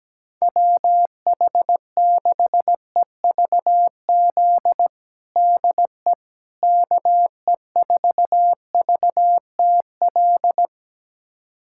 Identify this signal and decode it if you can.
Morse code, heard as a keyed tone, about 17 wpm, 700 Hz. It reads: WH6EVZ DE KE4VTL
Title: USA Callsign DE Callsign - 6 Characters - 17wpm